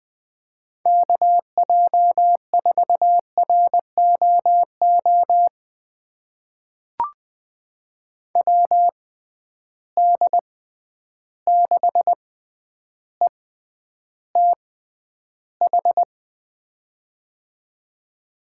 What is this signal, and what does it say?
Morse code, heard as a keyed tone, about 20 wpm, spaced as for 10 wpm, 700 Hz. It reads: KJ4ROO WD6ETH
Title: USA Call Signs - 6 Characters - 6x Character Spacing - 20wpm